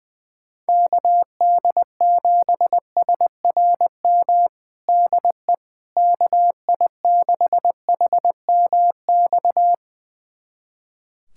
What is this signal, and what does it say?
Morse code, heard as a keyed tone, about 20 wpm, 700 Hz. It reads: KD7SRM DE KI6HMX
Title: USA Callsign DE Callsign 20wpm